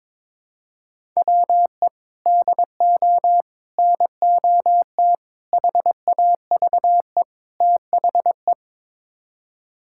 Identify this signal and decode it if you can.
Morse code, heard as a keyed tone, about 22 wpm, 700 Hz. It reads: WE DO NOT HAVE THE
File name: practice / 22wpm / Build A Phrase - 5-Word Phrases - 22wpm